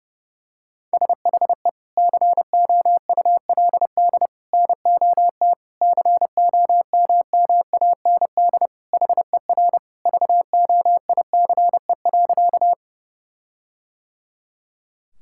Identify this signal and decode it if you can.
Morse code, heard as a keyed tone, about 32 wpm, 700 Hz. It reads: SHE COULD NOT COMMAND HER VOICE.
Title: Sentences from Top 1600 Words 30wpm